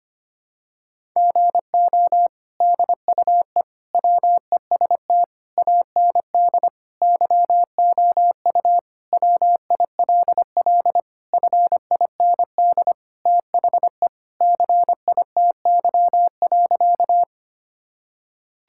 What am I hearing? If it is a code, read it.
Morse code, heard as a keyed tone, about 25 wpm, 700 Hz. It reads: GO DUE WEST AND YOU WILL FIND THE CITY.
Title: Sentences from Top 1000 Words - Review Part 6 - 25wpm